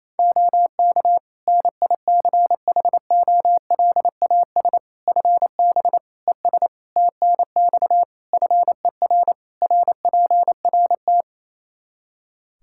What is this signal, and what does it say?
Morse code, heard as a keyed tone, about 28 wpm, 700 Hz. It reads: OK NICHOLAS FB ES TNX FER RPRT
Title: Practice QSO - Exchange 2 - Info - 28wpm